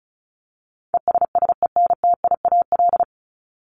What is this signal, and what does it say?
Morse code, heard as a keyed tone, about 34 wpm, 700 Hz. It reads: ESSENTIAL
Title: LY Suffix - Top 100 Words with Base Words 35wpm